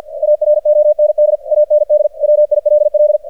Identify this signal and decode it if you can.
Morse code, heard as a keyed tone, about 49 wpm, 600 Hz. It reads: AROND AND WILL
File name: archived / CRSnet